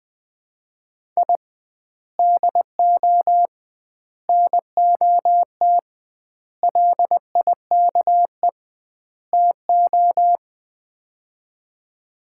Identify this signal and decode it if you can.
Morse code, heard as a keyed tone, about 20 wpm, 700 Hz. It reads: I DO NOT LIKE TO
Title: 2x Word-Spacing - Sets of 5 Words - 20wpm